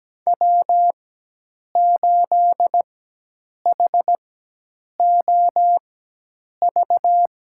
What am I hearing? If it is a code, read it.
Morse code, heard as a keyed tone, about 17 wpm, spaced as for 8 wpm, 700 Hz. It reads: W8HOV
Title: USA Call Signs - 5 Characters - 4x Character Spacing - 17wpm